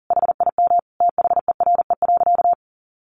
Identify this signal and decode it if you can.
Morse code, heard as a keyed tone, about 40 wpm, 700 Hz. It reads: HIM THERE.
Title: Sentences from Top 1100 Words 40wpm